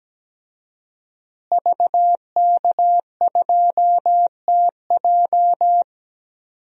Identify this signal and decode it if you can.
Morse code, heard as a keyed tone, about 17 wpm, 700 Hz. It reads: VK2TJ
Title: DXCC-100 Call Signs - 5 Characters - Repeated 2x - 17wpm